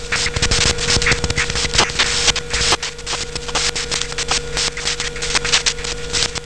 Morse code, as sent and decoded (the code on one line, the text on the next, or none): none